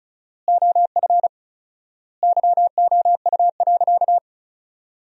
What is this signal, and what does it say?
Morse code, heard as a keyed tone, about 34 wpm, 700 Hz. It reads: OF YOU.
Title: 4x Word-Spacing - Sentences from Top 200 Words - 35wpm